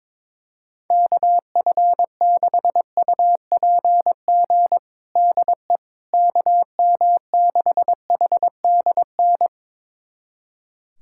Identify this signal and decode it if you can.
Morse code, heard as a keyed tone, about 22 wpm, 700 Hz. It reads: KF6UPG DE KM6HDN
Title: USA Callsign DE Callsign - 6 Characters - 22wpm